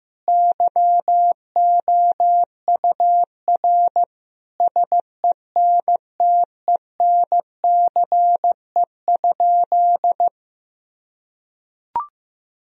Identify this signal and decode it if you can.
Morse code, heard as a keyed tone, about 15 wpm, 700 Hz. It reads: YOUR SENTENCE?
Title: Sentences from Top 1000 Words - Review Part 1 - 15wpm